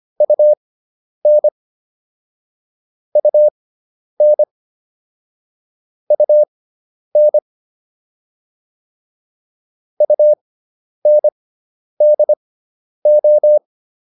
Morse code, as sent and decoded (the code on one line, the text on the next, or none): ..- -. / ..- -. / ..- -. / ..- -. -.. ---
UN UN UN UNDO